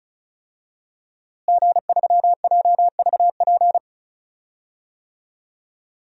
Morse code, heard as a keyed tone, about 36 wpm, 700 Hz. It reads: G3JVP
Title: DXCC-100 Call Signs - 5 Characters - Repeated 2x - 35wpm